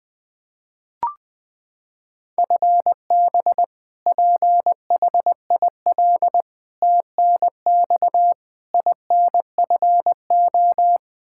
Morse code, heard as a keyed tone, about 21 wpm, 700 Hz. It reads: FB PHIL TNX INFO
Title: Practice QSO - Exchange 3 - More Info - 20wpm